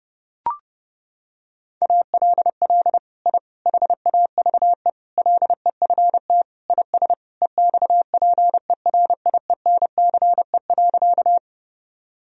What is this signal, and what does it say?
Morse code, heard as a keyed tone, about 32 wpm, 700 Hz. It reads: ALL I HAVE LEFT IS EXPERIENCE.
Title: Sentences from Top 1000 Words - Review Part 3 - 30wpm